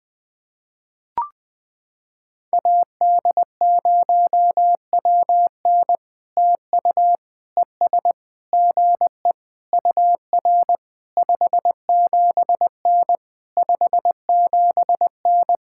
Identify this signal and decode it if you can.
Morse code, heard as a keyed tone, about 20 wpm, 700 Hz. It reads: AD0WN TU ES GE UR 57N 57N
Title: POTA QSO Practice - 20wpm